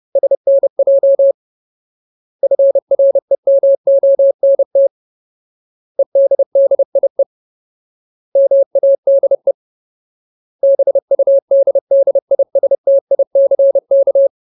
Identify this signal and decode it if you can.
Morse code, heard as a keyed tone, about 30 wpm, 550 Hz. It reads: SNJ FREMONT EDDIE MADE BUDDISTICK
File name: Ultimate Word List / 30WPM Ultimate